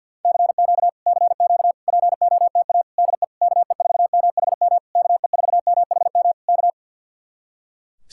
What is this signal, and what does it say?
Morse code, heard as a keyed tone, about 64 wpm, 700 Hz. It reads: CQ CQ POTA DE KE4MHM KE4MHM K